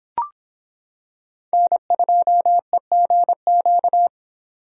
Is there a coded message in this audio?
Yes: Morse code from 1.5 to 4.1 s, heard as a keyed tone, about 25 wpm, 700 Hz. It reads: N2EGQ